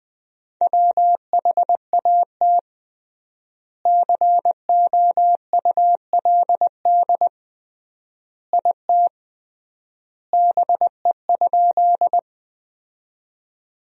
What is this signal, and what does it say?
Morse code, heard as a keyed tone, about 20 wpm, 700 Hz. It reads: WHAT COULD IT BE?
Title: 3x Word-Spacing - Sentences from Top 100 Words - 20wpm